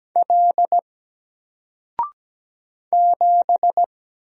Morse code, heard as a keyed tone, about 17 wpm, 700 Hz. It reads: L 7
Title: Single Letter-Number Alt 17wpm